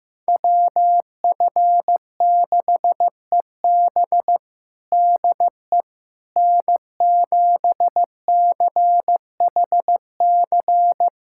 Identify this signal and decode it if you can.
Morse code, heard as a keyed tone, about 15 wpm, 700 Hz. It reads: WF6EB DE N7CHC
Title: USA Callsign DE Callsign - 5 Characters - 15wpm